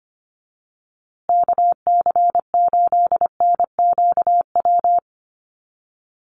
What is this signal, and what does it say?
Morse code, heard as a keyed tone, about 25 wpm, 700 Hz. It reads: KC8NQW
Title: USA Call Signs Repeated 3x 25wpm